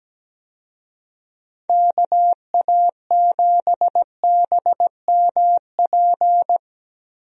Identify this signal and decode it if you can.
Morse code, heard as a keyed tone, about 17 wpm, 700 Hz. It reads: KA7BMP
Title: USA Call Signs - 6 Characters - 17wpm